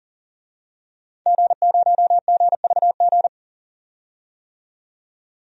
Morse code, heard as a keyed tone, about 40 wpm, 700 Hz. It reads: G0GVG
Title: DXCC-100 Call Signs - 5 Characters - 40wpm